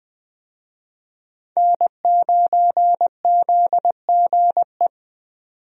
Morse code, heard as a keyed tone, about 20 wpm, 700 Hz. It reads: N9ZGE
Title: USA Call Signs - 5 Characters - 20wpm